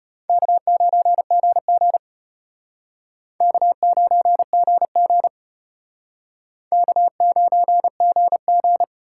Morse code, heard as a keyed tone, about 32 wpm, 700 Hz. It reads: K9GG K9GG K9GG